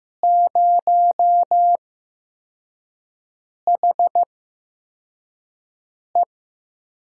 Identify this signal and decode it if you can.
Morse code, heard as a keyed tone, about 15 wpm, spaced as for 4 wpm, 700 Hz. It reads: 0HE
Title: USA Call Signs - 4 Characters - 8x Character Spacing - 15wpm